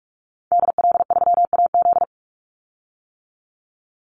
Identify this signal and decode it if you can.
Morse code, heard as a keyed tone, about 44 wpm, 700 Hz. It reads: DL3AZ
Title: Call Signs - 45wpm